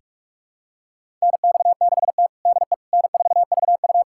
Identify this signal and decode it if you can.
Morse code, heard as a keyed tone, about 46 wpm, 700 Hz. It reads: NX6T DE N4VU
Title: USA Callsign DE Callsign - 4 Characters - 45wpm